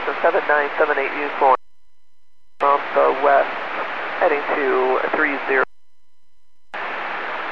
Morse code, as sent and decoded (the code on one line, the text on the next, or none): none